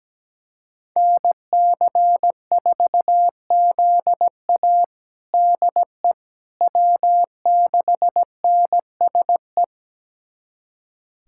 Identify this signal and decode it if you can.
Morse code, heard as a keyed tone, about 17 wpm, 700 Hz. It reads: NC4ZA DE W6NSE
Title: USA Callsign DE Callsign - 5 Characters - 17wpm